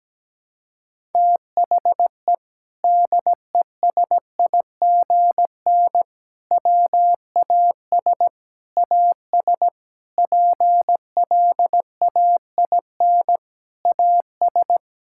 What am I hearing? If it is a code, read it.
Morse code, heard as a keyed tone, about 17 wpm, 700 Hz. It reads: THE DESIGN WAS AS PLAIN AS
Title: Sentences from Top 1000 Words - Review Part 5 - 17wpm